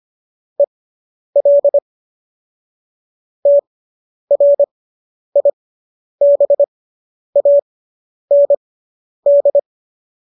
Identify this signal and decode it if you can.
Morse code, heard as a keyed tone, about 25 wpm, spaced as for 10 wpm, 570 Hz. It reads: EL TRIBAND
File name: qso219 10